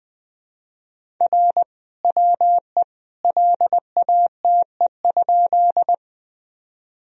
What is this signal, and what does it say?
Morse code, heard as a keyed tone, about 21 wpm, 700 Hz. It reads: R WE LATE?
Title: Sentences from Top 300 Words and Abbreviations 20wpm